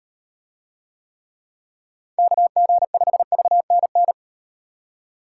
Morse code, heard as a keyed tone, about 36 wpm, 700 Hz. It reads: KG5VNN